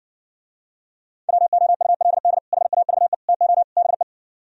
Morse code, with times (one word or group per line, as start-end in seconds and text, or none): 1.3-2.4 WOULD
2.5-3.1 HAVE
3.3-3.6 TO
3.8-4.0 BE